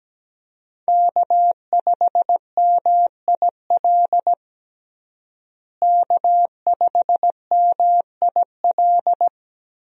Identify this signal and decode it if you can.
Morse code, heard as a keyed tone, about 17 wpm, 700 Hz. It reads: K5MIL K5MIL
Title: USA Call Signs - 5 Characters - Repeated 2x - 17wpm